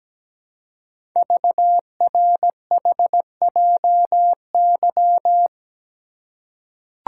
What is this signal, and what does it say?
Morse code, heard as a keyed tone, about 17 wpm, 700 Hz. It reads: VRHJY
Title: Code Groups - 5 Letter:Numbers - 17wpm